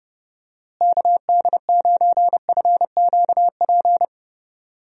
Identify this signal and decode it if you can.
Morse code, heard as a keyed tone, about 32 wpm, 700 Hz. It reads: KD9FQP
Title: USA Call Signs - 6 Characters - Repeated 2x - 30wpm